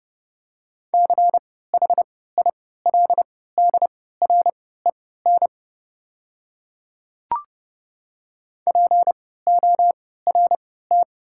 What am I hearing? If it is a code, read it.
Morse code, heard as a keyed tone, about 32 wpm, spaced as for 17 wpm, 700 Hz. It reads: CHILDREN PORT
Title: 3x Character Spacing - Top 300 Words - 30wpm